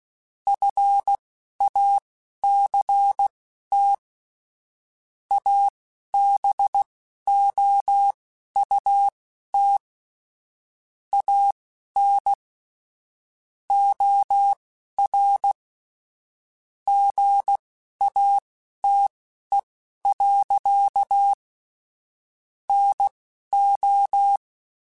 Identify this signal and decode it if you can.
Morse code, heard as a keyed tone, about 16 wpm, spaced as for 11 wpm, 800 Hz. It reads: FACT ABOUT AN OR GATE. NO
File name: Technical Subjects / Technical Subjects-mp3s / OR Gate @10